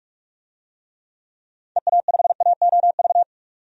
Morse code, heard as a keyed tone, about 41 wpm, 700 Hz. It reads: EA5AOV
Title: DXCC-100 Call Signs - 6 Characters - 45wpm